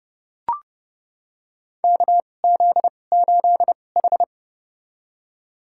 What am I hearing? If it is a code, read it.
Morse code, heard as a keyed tone, about 30 wpm, spaced as for 22 wpm, 700 Hz. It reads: KZ8H
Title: USA Call Signs - 2x Character Spacing - 30wpm